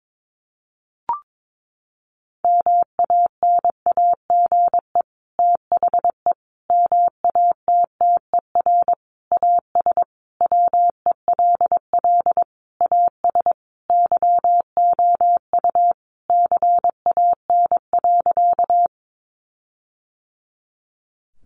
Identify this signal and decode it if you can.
Morse code, heard as a keyed tone, about 22 wpm, 700 Hz. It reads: MANAGE THE MATTER AS WELL AS YOU CAN.
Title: Sentences from Top 1400 Words 22wpm